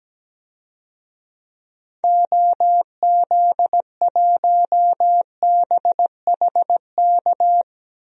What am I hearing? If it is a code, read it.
Morse code, heard as a keyed tone, about 17 wpm, 700 Hz. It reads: OZ1BHK
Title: DXCC-100 Call Signs - 6 Characters - Repeated 3x - 17wpm